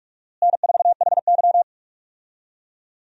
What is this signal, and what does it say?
Morse code, heard as a keyed tone, about 46 wpm, 700 Hz. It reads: N4HY
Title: USA Call Signs - 4 Characters - Repeated 2x - 45wpm